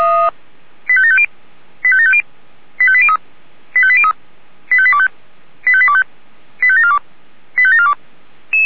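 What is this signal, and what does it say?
Five-tone selective call (ZVEI), heard as quick runs of five tones, tones of about 70 ms. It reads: 85750 85750 85803 85803 85825 85825 85742 85742
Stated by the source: ZVEI five-tone selective call received on an SDR.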